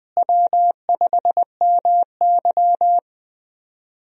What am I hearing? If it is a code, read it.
Morse code, heard as a keyed tone, about 20 wpm, 700 Hz. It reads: W5MY
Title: USA Call Signs - 4 Characters - Repeated 3x - 20wpm